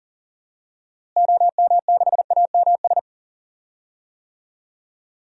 Morse code, heard as a keyed tone, about 40 wpm, 700 Hz. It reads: OM6AMS
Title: DXCC-100 Call Signs - 6 Characters - Repeated 2x - 40wpm